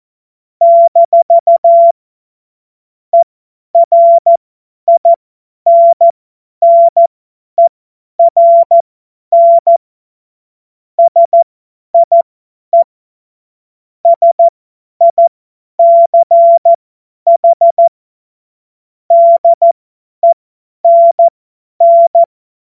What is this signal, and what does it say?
Morse code, heard as a keyed tone, about 14 wpm, spaced as for 10 wpm, 680 Hz. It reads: - ERINNERN SIE SICH DENN
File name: Morsekurs / Witze / Witz 23. August 2022 50 F70